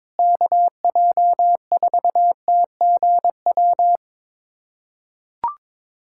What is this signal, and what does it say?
Morse code, heard as a keyed tone, about 22 wpm, 700 Hz. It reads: KJ4TGW
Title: USA Callsign DE Callsign - 6 Characters - 22wpm